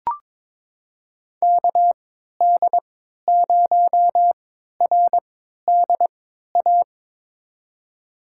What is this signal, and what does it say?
Morse code, heard as a keyed tone, about 21 wpm, spaced as for 12 wpm, 700 Hz. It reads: KD0RDA
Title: USA Call Signs - 6 Characters - 3x Character Spacing - 22wpm